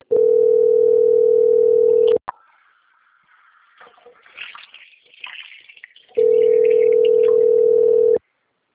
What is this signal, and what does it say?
Ringback tone (North America), 440 + 480 Hz.
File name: out- - - - - .3587